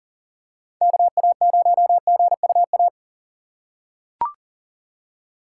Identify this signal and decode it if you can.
Morse code, heard as a keyed tone, about 40 wpm, 700 Hz. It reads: KA0GUA
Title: USA Call Signs - 6 Characters - 6x Character Spacing - 40wpm